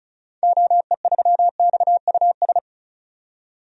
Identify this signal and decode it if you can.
Morse code, heard as a keyed tone, about 34 wpm, 700 Hz. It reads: OE3XUS
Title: DXCC-100 Call Signs - 6 Characters - 35wpm